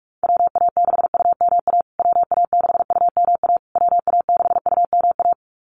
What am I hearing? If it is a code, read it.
Morse code, heard as a keyed tone, about 44 wpm, 700 Hz. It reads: WA6UMA WA6UMA WA6UMA
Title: USA Call Signs Repeated 3x 45wpm